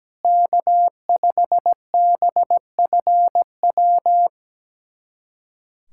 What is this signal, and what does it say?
Morse code, heard as a keyed tone, about 17 wpm, 700 Hz. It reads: K5BFW
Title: USA Callsign DE Callsign 17wpm